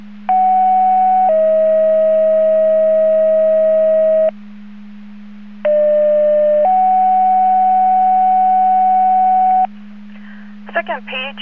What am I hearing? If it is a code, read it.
Two-tone page, heard as a one-second tone, then a three-second tone, about 750 then 640 Hz, about 600 then 750 Hz, with speech. It is heard from 0.3 to 9.7 s.